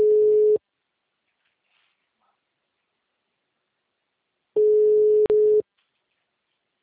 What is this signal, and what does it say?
Ringback tone (Europe, 425 Hz).
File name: out- -2014